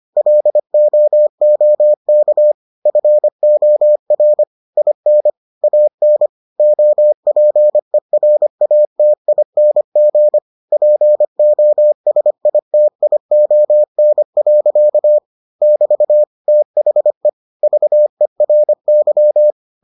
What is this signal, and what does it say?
Morse code, heard as a keyed tone, about 25 wpm, 600 Hz. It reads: LOOK FOR IN AN OPERATING POSITION. <BT> THE VERY